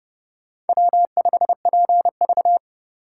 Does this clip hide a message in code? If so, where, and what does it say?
Yes: Morse code from 0.7 to 2.6 s, heard as a keyed tone, about 27 wpm, 700 Hz. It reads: W5PV